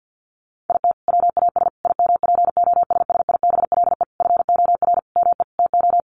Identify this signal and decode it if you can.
Morse code, heard as a keyed tone, about 50 wpm, 700 Hz. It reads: IT WAS IMPOSSIBLE FOR ME TO
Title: Sentences from Top 1100 Words 50wpm